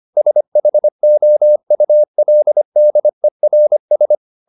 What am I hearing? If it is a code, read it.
Morse code, heard as a keyed tone, about 25 wpm, 600 Hz. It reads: SHOULDERS
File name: the old man and the sea part 1 25wpm